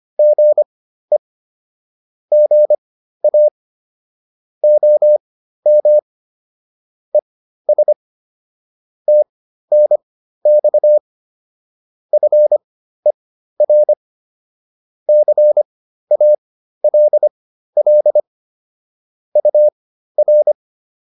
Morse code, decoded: GE GA OM ES TNX FER CALL UR

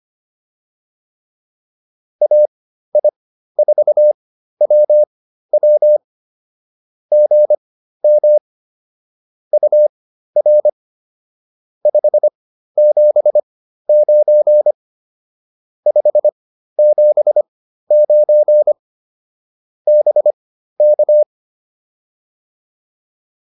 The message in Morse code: AI4WW GM UR 579 579 BK

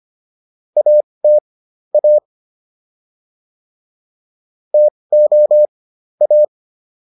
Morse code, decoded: AT A TO A

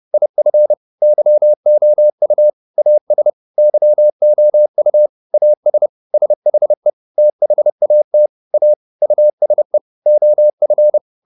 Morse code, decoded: IF YOU AS YOU AS SHE THAT A USE OF

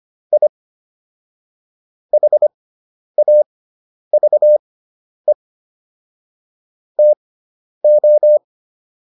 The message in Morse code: I HAVE TO

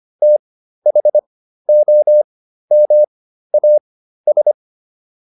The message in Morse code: THOMAS

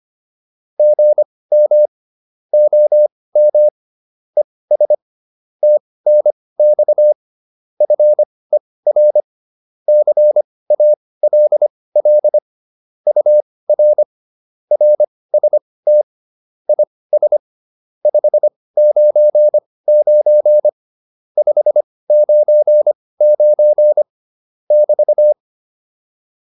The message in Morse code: GM OM ES TNX FER CALL UR RST IS 599 599 <BT>

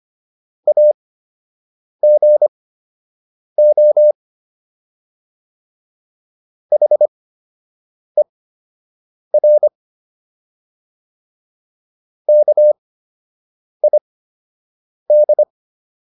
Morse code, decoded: AGO HER KID

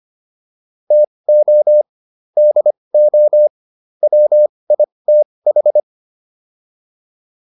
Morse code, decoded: TO DO WITH

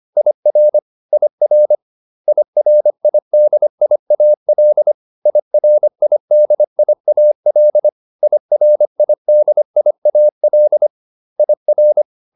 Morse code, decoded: IR IR IRIDIAL IRIDIAL IRIDIAL IR

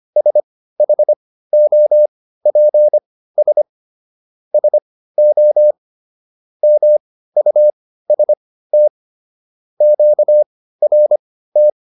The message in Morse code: SHOPS SO MUST QRT

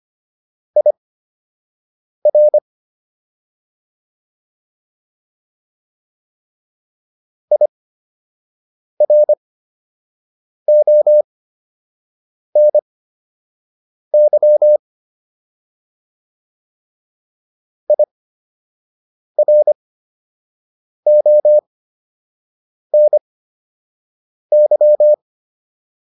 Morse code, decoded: IR IRONY IRONY

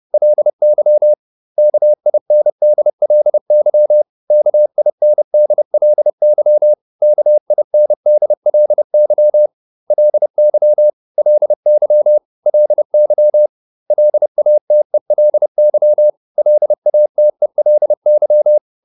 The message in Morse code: LY KINDLY KINDLY KINDLY LY LY LY LATELY LATELY

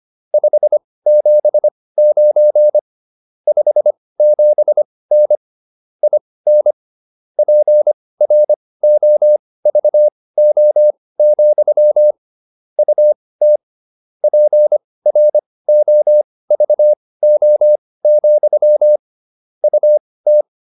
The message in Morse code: 579 57N IN PROVO, UT PROVO, UT

